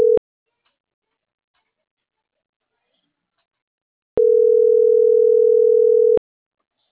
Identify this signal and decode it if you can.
Ringback tone (North America), 440 + 480 Hz.